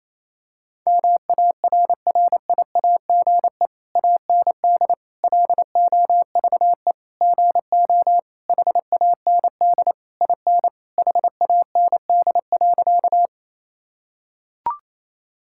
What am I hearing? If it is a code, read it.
Morse code, heard as a keyed tone, about 28 wpm, 700 Hz. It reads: MARRIAGE AND LOVE GO HAND IN HAND.
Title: Sentences from Top 1000 Words - Review Part 5 - 28wpm